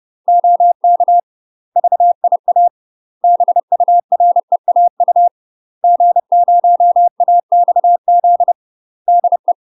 Morse code, heard as a keyed tone, about 30 wpm, 700 Hz. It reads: OK VIA BUREAU G0AXZ DE
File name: UK MASTER QSOs TRAINING/30WPM